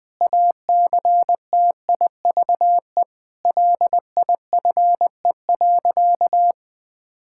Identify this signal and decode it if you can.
Morse code, heard as a keyed tone, about 20 wpm, 700 Hz. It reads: ACTIVE LIFE.
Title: Sentences from Top 1000 Words - Review Part 6 - 20wpm